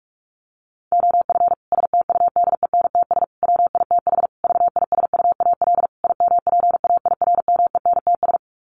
Morse code, heard as a keyed tone, about 44 wpm, 700 Hz. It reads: OF STUDENTS WITH VISUAL IMPAIRMENTS
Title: Sets of 5 Words Encore 45wpm